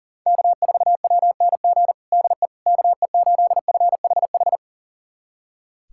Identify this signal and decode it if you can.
Morse code, heard as a keyed tone, about 40 wpm, 700 Hz. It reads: K4WNG DE KE8FHH
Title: USA Callsign DE Callsign 40wpm